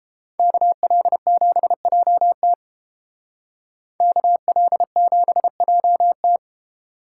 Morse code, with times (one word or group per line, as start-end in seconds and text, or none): 0.4-2.5 KL7JT
4.0-6.4 KL7JT